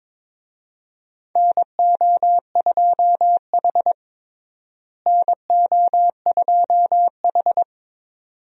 Morse code, heard as a keyed tone, about 22 wpm, 700 Hz. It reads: NO2H NO2H
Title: USA Call Signs - 4 Characters - Repeated 2x - 22wpm